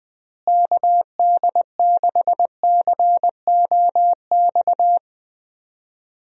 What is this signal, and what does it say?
Morse code, heard as a keyed tone, about 21 wpm, 700 Hz. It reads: KD6COX
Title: USA Call Signs - 6 Characters - Repeated 2x - 20wpm